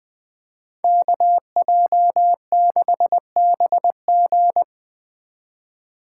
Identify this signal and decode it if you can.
Morse code, heard as a keyed tone, about 20 wpm, 700 Hz. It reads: KJ6BG
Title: Speed Racing - USA 2x2 Call Signs 17wpm